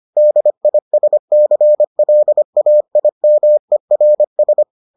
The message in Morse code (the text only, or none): DISCLAIMERS